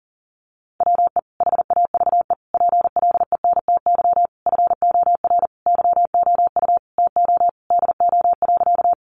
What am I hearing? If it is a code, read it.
Morse code, heard as a keyed tone, about 40 wpm, 700 Hz. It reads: WE HAVE PLENTY FOR YOU TO DO.
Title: Sentences from Top 1300 Words 40wpm